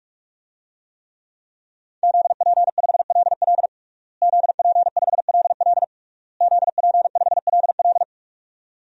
Morse code, heard as a keyed tone, about 41 wpm, 700 Hz. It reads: ZP5LL ZP5LL ZP5LL
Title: DXCC-100 Call Signs - 5 Characters - Repeated 3x - 45wpm